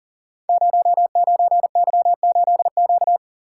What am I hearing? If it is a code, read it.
Morse code, heard as a keyed tone, about 41 wpm, 700 Hz. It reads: 09Y8Q
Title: Code Groups - 5 Letter:Numbers - 40wpm